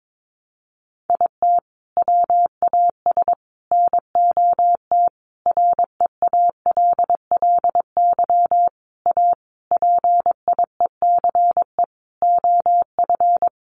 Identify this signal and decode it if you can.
Morse code, heard as a keyed tone, about 22 wpm, 700 Hz. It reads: IT WAS NOT REALLY A PIECE OF
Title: Sentences from Top 600 Words 22wpm